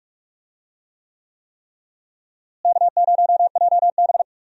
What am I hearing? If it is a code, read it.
Morse code, heard as a keyed tone, about 46 wpm, 700 Hz. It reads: K0JB
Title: USA Call Signs - 4 Characters - Repeated 3x - 45wpm